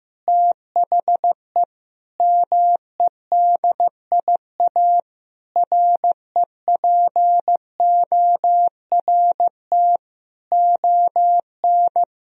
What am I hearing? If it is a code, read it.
Morse code, heard as a keyed tone, about 15 wpm, 700 Hz. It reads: THE MEDIA REPORT ON
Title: Sentences from Top 1000 Words - Review Part 4 - 15wpm